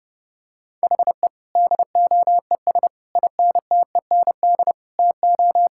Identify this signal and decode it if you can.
Morse code, heard as a keyed tone, about 30 wpm, 700 Hz. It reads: HE DOES INTEND TO
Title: Sentences from Top 1600 Words 30wpm